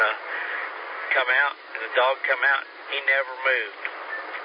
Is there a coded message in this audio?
Yes: Emergency Alert System (SAME) tones from 1.0 to 3.9 s, heard as harsh buzzing bursts, with speech.